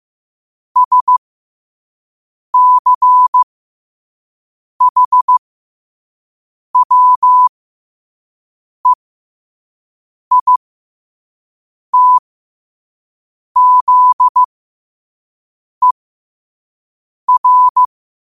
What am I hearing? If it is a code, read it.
Morse code, heard as a keyed tone, about 15 wpm, spaced as for 5 wpm, 1000 Hz. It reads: SCHWEITZER
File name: qotdfiles / 5wpm